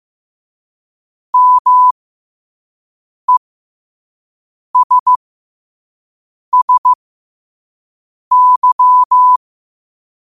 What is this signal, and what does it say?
Morse code, heard as a keyed tone, about 15 wpm, spaced as for 5 wpm, 1000 Hz. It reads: MESSY
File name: qotdfiles / 5wpm